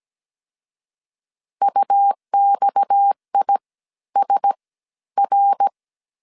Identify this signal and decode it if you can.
Morse code, heard as a keyed tone, about 17 wpm, 800 Hz. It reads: UXI S R
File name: 16wpm